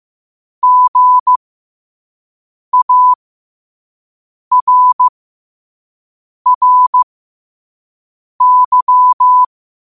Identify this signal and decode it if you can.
Morse code, heard as a keyed tone, about 15 wpm, spaced as for 5 wpm, 1000 Hz. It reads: GARRY